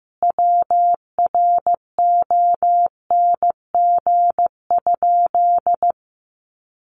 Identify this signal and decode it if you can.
Morse code, heard as a keyed tone, about 15 wpm, 700 Hz. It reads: WRONG?
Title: Sentences from Top 1100 Words 15wpm